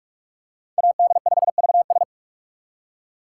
Morse code, heard as a keyed tone, about 46 wpm, 700 Hz. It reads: AD5VS
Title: USA Call Signs - 2x Character Spacing - 45wpm